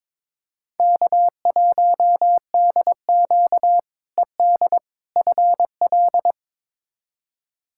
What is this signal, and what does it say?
Morse code, heard as a keyed tone, about 22 wpm, 700 Hz. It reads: K1DQ ED FL
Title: K1USN SST QSO Practice - 22wpm